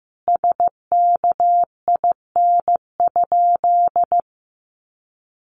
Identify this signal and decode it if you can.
Morse code, heard as a keyed tone, about 15 wpm, 700 Hz. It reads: SKIN?
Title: Sentences from Top 1300 Words 15wpm